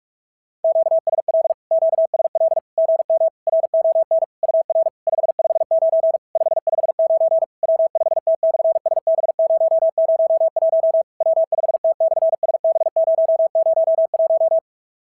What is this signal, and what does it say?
Morse code, heard as a keyed tone, about 46 wpm, 650 Hz. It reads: QSL QSL GM RON UR 559 559 W5T/SB001 W5T/SB001
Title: SOTA QSO Practice 45wpm